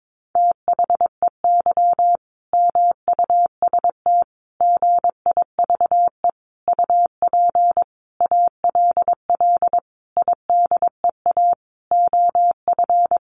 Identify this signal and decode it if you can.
Morse code, heard as a keyed tone, about 22 wpm, 700 Hz. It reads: THEY MUST GIVE UP ALL IDEA OF
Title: Sentences from Top 1100 Words 22wpm